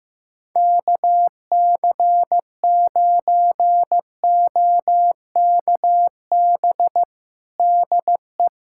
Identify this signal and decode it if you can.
Morse code, heard as a keyed tone, about 15 wpm, 700 Hz. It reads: KC9OKB DE
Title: USA Callsign DE Callsign - 6 Characters - 15wpm